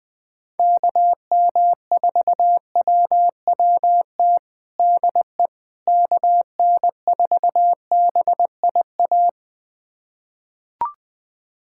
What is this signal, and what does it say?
Morse code, heard as a keyed tone, about 20 wpm, 700 Hz. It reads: KM4WWT DE KN4BIA
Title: USA Callsign DE Callsign 20wpm